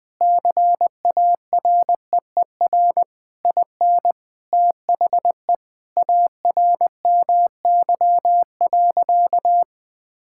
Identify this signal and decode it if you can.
Morse code, heard as a keyed tone, about 20 wpm, 700 Hz. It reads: CAREER IN THE ARMY.
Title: Sentences from Top 1000 Words - Review Part 6 - 20wpm